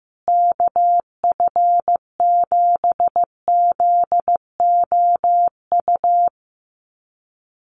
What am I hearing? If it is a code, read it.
Morse code, heard as a keyed tone, about 15 wpm, 700 Hz. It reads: KF7ZOU
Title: USA Call Signs Repeated 3x 15wpm